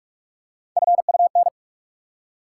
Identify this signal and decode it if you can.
Morse code, heard as a keyed tone, about 46 wpm, 700 Hz. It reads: FUN
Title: Three Letter Words - 45wpm